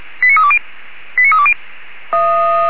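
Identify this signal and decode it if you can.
Five-tone selective call (ZVEI), heard as quick runs of five tones, tones of about 80 ms. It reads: 89320 89320
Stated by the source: ZVEI five-tone selective call 89315, received by SDR, from the Feuerwehr Orlamünde audio server.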